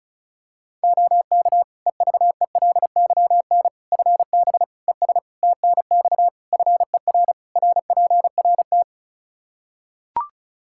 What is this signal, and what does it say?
Morse code, heard as a keyed tone, about 36 wpm, 700 Hz. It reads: OK EVELYN FB ES TNX FER RPRT
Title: Practice QSO - Exchange 2 - Info - 35wpm